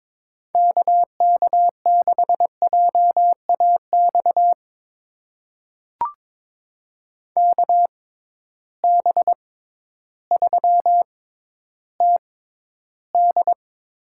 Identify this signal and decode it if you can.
Morse code, heard as a keyed tone, about 22 wpm, spaced as for 12 wpm, 700 Hz. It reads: KK6JAX KB3TD
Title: USA Call Signs - 6 Characters - 6x Character Spacing - 22wpm